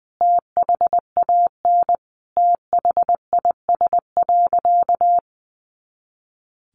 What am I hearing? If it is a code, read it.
Morse code, heard as a keyed tone, about 20 wpm, 700 Hz. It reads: THAN THIS.
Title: Sentences from Top 100 Words 20wpm